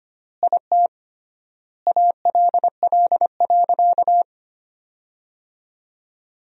Morse code, heard as a keyed tone, about 25 wpm, 700 Hz. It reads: IT ALL.
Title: 3x Word-Spacing - Sentences from Top 200 Words - 25wpm